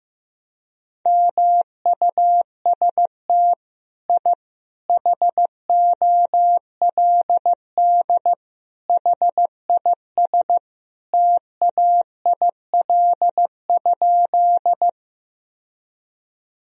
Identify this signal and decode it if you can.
Morse code, heard as a keyed tone, about 15 wpm, 700 Hz. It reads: MUST I HOLD HIS TAIL?
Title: Sentences from Top 1000 Words - Review Part 3 - 15wpm